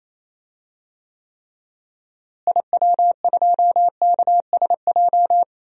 Morse code, heard as a keyed tone, about 28 wpm, 700 Hz. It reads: IW2KSJ
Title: DXCC-100 Call Signs - 6 Characters - Repeated 2x - 28wpm